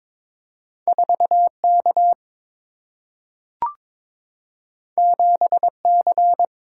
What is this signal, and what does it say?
Morse code, heard as a keyed tone, about 22 wpm, 700 Hz. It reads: 4 K 7 C
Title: Single-Letter Single-Number 22wpm